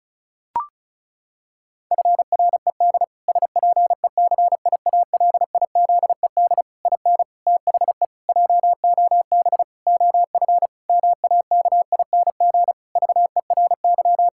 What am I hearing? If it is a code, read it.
Morse code, heard as a keyed tone, about 34 wpm, 700 Hz. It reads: FRED SPECIALIZED IN THE JOB OF MAKING VERY
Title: Pangram Challenge v2 - 35wpm